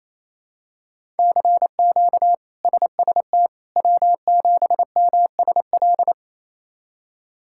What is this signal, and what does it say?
Morse code, heard as a keyed tone, about 28 wpm, 700 Hz. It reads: CQ SST W7MSL
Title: K1USN SST QSO Practice - 28wpm